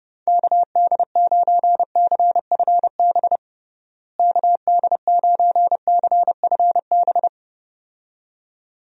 Morse code, heard as a keyed tone, about 30 wpm, 700 Hz. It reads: KD9CFB KD9CFB
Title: USA Call Signs - 6 Characters - Repeated 2x - 30wpm